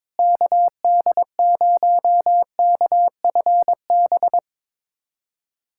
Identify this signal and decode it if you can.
Morse code, heard as a keyed tone, about 22 wpm, 700 Hz. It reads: KD0KFB
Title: USA Callsign DE Callsign - 6 Characters - 22wpm